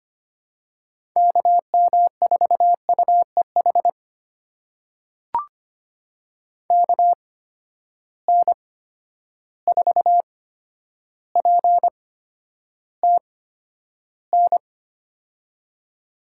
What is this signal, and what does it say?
Morse code, heard as a keyed tone, about 25 wpm, spaced as for 11 wpm, 700 Hz. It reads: KM4UEH KN4PTN
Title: USA Call Signs - 6 Characters - 8x Character Spacing - 25wpm